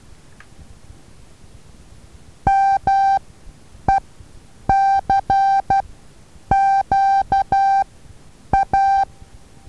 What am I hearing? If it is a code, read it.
Morse code, heard as a keyed tone, about 12 wpm, spaced as for 8 wpm, 780 Hz. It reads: MECQA
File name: Taviro HA6VA A